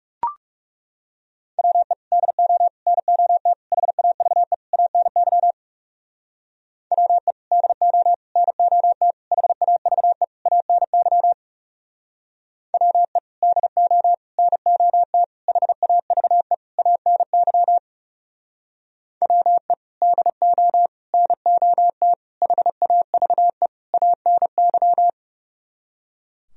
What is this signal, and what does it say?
Morse code, heard as a keyed tone, about 37 wpm, 700 Hz. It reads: WE DO NOT HAVE ANY WE DO NOT HAVE ANY WE DO NOT HAVE ANY WE DO NOT HAVE ANY
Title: Speed-Racing - Sets of 5 Words 30wpm